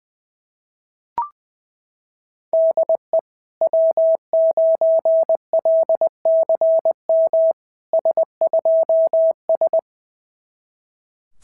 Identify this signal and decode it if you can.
Morse code, heard as a keyed tone, about 21 wpm, 650 Hz. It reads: DE W9LCM S2S
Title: SOTA QSO Practice 20wpm